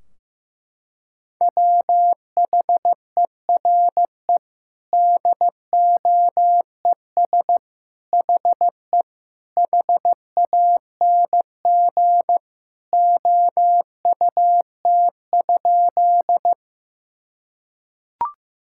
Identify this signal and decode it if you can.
Morse code, heard as a keyed tone, about 15 wpm, 700 Hz. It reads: WHERE DOES HE HANG OUT?
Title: Sentences from Top 1600 Words 15wpm